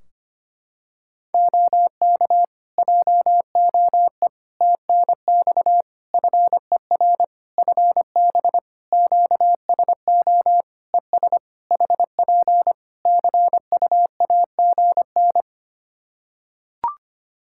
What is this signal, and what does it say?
Morse code, heard as a keyed tone, about 25 wpm, 700 Hz. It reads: OK JOE TNX FER FB QSO ES HP CUAGN
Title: Practice QSO - Exchange 4 - Ending - 25wpm